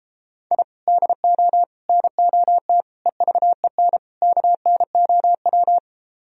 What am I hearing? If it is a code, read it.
Morse code, heard as a keyed tone, about 33 wpm, 700 Hz. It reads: I DO NOT EVEN KNOW